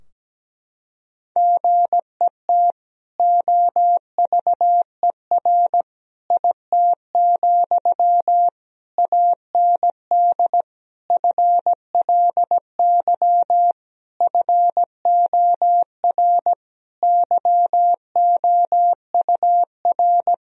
Morse code, heard as a keyed tone, about 17 wpm, 700 Hz. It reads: GET OVER IT, AND FLY FOR YOUR
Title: Sentences from Top 1000 Words - Review Part 2 - 17wpm